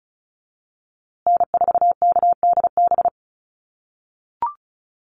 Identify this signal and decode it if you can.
Morse code, heard as a keyed tone, about 34 wpm, 700 Hz. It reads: N4KDB
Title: USA Call Signs Repeated 2x 35wpm